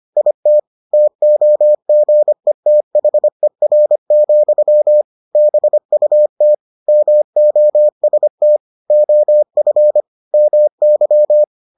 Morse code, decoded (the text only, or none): IT TOGETHER, BUT MOST OF MY